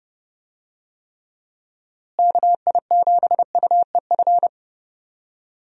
Morse code, heard as a keyed tone, about 30 wpm, 700 Hz. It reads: KI7UEF
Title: USA Call Signs - 6 Characters - Repeated 3x - 30wpm